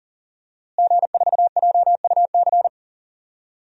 Morse code, heard as a keyed tone, about 40 wpm, 700 Hz. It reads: G4JUC
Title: DXCC-100 Call Signs - 5 Characters - Repeated 3x - 40wpm